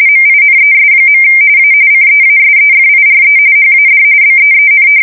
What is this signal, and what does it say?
RTTY, about 45.45 baud. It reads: MINES. A CEASEFIRE HAD BEEN
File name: fsk170 / battleshipnj